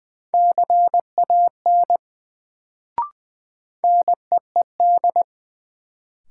Morse code, heard as a keyed tone, about 20 wpm, 700 Hz. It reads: CAN NEED
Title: Sets of 2 Words - New Word Drill - 20wpm